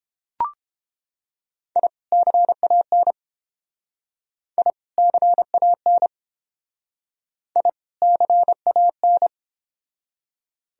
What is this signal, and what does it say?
Morse code, heard as a keyed tone, about 29 wpm, 700 Hz. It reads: I CAN I CAN I CAN